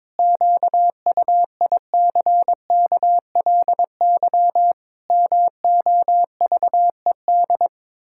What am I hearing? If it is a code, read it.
Morse code, heard as a keyed tone, about 21 wpm, 700 Hz. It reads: QUICKLY MOVED
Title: Pangram Challenge v2 - 22wpm